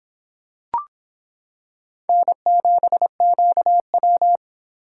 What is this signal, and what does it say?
Morse code, heard as a keyed tone, about 25 wpm, 700 Hz. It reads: N7QW